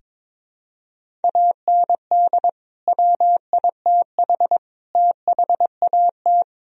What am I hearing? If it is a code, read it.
Morse code, heard as a keyed tone, about 21 wpm, 700 Hz. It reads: AND WITH THAT
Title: Sentences from Top 1000 Words - Review Part 1 - 22wpm